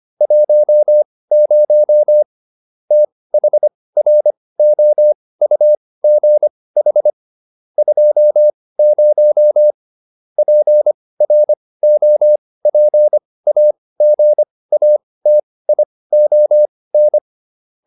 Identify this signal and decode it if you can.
Morse code, heard as a keyed tone, about 25 wpm, spaced as for 18 wpm, 600 Hz. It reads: 10 THROUGH 20 PROPAGATION